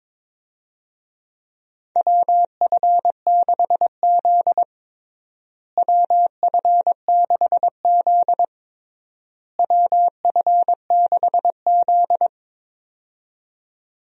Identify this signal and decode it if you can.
Morse code, heard as a keyed tone, about 22 wpm, 700 Hz. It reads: WF6Z WF6Z WF6Z
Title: USA Call Signs - 4 Characters - Repeated 3x - 22wpm